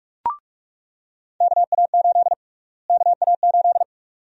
Morse code, heard as a keyed tone, about 46 wpm, 700 Hz. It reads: KA8 KA8
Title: USA Call Sign Prefixes - Repeated 2x - 45wpm